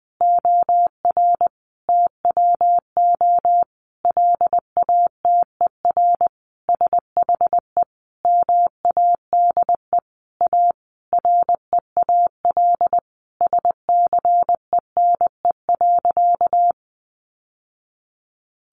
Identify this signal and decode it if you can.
Morse code, heard as a keyed tone, about 20 wpm, 700 Hz. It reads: OR TWO LATER SHE MADE A REAL SCENE.
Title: Sentences from Top 1100 Words 20wpm